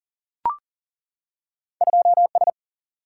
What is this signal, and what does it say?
Morse code, heard as a keyed tone, about 42 wpm, 700 Hz. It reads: 2 S
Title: Single-Letter Single-Number 40wpm